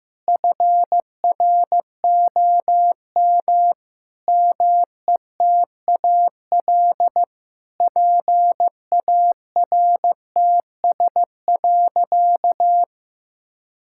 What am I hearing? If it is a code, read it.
Morse code, heard as a keyed tone, about 15 wpm, 700 Hz. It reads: FROM METAL PARTS.
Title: Sentences from Top 1000 Words - Review Part 6 - 15wpm